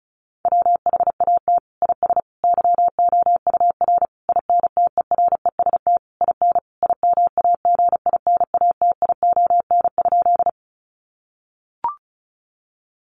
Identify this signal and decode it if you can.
Morse code, heard as a keyed tone, about 34 wpm, 700 Hz. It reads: WHAT IS YOUR INTEREST IN IMAGINATION?
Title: Sentences from Top 1300 Words 35wpm